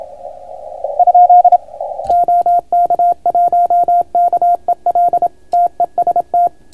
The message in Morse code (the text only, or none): ? OK1KEL TEST